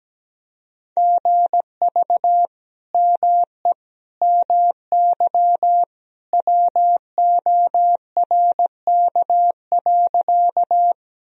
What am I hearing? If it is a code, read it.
Morse code, heard as a keyed tone, about 17 wpm, 700 Hz. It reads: GV ME MY WORK.
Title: Sentences from Top 200 Words and Abbreviations 17wpm